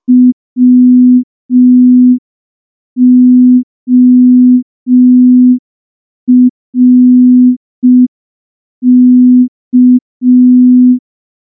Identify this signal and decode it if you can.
Morse code, heard as a keyed tone, about 5 wpm, 250 Hz. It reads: WORK
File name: morse / morse-training-05WPM